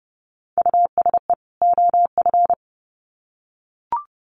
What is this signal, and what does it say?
Morse code, heard as a keyed tone, about 30 wpm, 700 Hz. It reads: USE OF
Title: Sets of 2 Words 30wpm